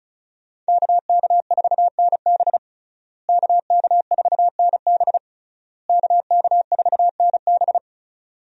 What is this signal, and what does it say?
Morse code, heard as a keyed tone, about 36 wpm, 700 Hz. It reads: KK4NB KK4NB KK4NB
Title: USA Call Signs - 5 Characters - Repeated 3x - 35wpm